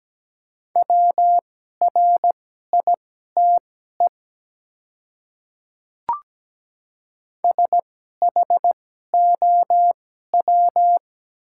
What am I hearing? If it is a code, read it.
Morse code, heard as a keyed tone, about 17 wpm, spaced as for 12 wpm, 700 Hz. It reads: WRITE SHOW
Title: 2x Character Spacing - Top 200 Words - 17wpm